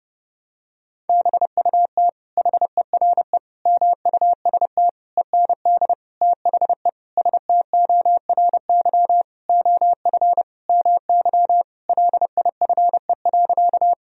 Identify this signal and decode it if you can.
Morse code, heard as a keyed tone, about 30 wpm, 700 Hz. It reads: BUT HERE MUST END THE STORY OF MY LIFE.
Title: Sentences from Top 1000 Words - Review Part 1 - 30wpm